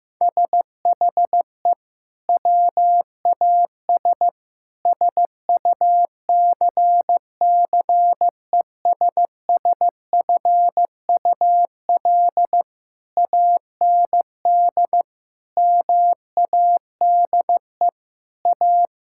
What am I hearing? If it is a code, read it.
Morse code, heard as a keyed tone, about 16 wpm, 700 Hz. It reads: SHE WAS SUCCESSFUL AND MADE A
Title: Sentences from Top 1000 Words - Review Part 6 - 15wpm